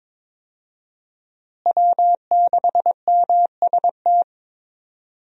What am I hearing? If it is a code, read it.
Morse code, heard as a keyed tone, about 21 wpm, 700 Hz. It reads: W6MST
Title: USA Call Signs - 5 Characters - Repeated 3x - 22wpm